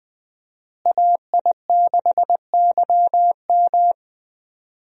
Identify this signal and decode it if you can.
Morse code, heard as a keyed tone, about 20 wpm, 700 Hz. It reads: AI6YM
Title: USA Call Signs - 5 Characters - Repeated 3x - 20wpm